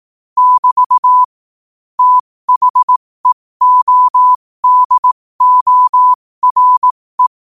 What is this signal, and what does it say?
Morse code, heard as a keyed tone, about 18 wpm, spaced as for 15 wpm, 1000 Hz. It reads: <BT> THEODORE
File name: qotdfiles / 15wpm